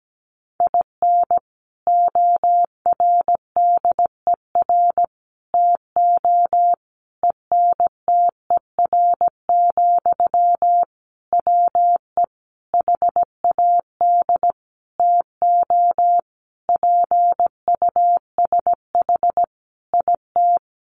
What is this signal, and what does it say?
Morse code, heard as a keyed tone, about 17 wpm, 700 Hz. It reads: IN ORDER TO ENTER, WE HAD TO PUSH IT
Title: Sentences from Top 1300 Words 17wpm